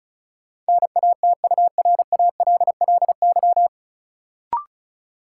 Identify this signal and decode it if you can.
Morse code, heard as a keyed tone, about 36 wpm, 700 Hz. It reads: NATURALLY
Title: Sentences from Top 1300 Words - New Word Drill Alt - 35wpm